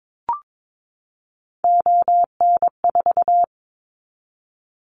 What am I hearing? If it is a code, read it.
Morse code, heard as a keyed tone, about 22 wpm, 700 Hz. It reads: ON4
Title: Callsign Prefixes for Top 50 DXCC - 22wpm